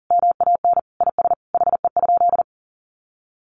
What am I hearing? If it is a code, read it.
Morse code, heard as a keyed tone, about 40 wpm, 700 Hz. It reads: MAN IS HE?
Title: Sentences from Top 1300 Words 40wpm